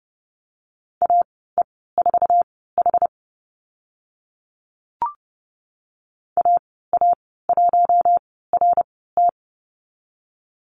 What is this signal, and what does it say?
Morse code, heard as a keyed tone, about 30 wpm, spaced as for 17 wpm, 700 Hz. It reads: AE4H AA1RT
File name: Lesson 15g - Number 5 - Call Signs 30wpm